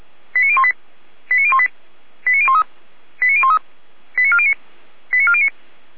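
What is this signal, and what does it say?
Five-tone selective call (ZVEI), heard as quick runs of five tones, tones of about 70 ms. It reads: 89018 89018 89013 89013 89409 89409
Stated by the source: ZVEI five-tone selective call received on SDR.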